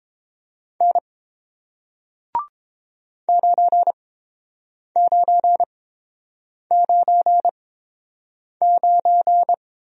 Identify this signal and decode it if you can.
Morse code, heard as a keyed tone, about 29 wpm, spaced as for 7 wpm, 700 Hz. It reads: N 9999